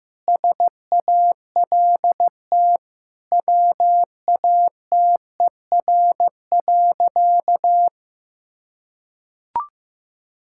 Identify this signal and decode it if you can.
Morse code, heard as a keyed tone, about 16 wpm, 700 Hz. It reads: SALT WATER.
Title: Sentences from Top 1000 Words - Review Part 5 - 15wpm